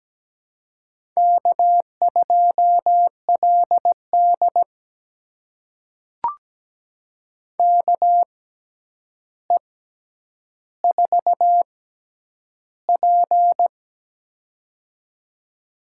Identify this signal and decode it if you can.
Morse code, heard as a keyed tone, about 17 wpm, spaced as for 9 wpm, 700 Hz. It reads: K2LD KE4P
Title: USA Call Signs - 4 Characters - 6x Character Spacing - 17wpm